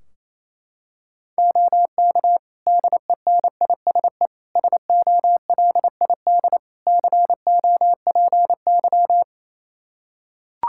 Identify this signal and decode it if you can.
Morse code, heard as a keyed tone, about 28 wpm, 700 Hz. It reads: OK DENISE SOLID COPY
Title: Practice QSO - Exchange 3 - More Info - 28wpm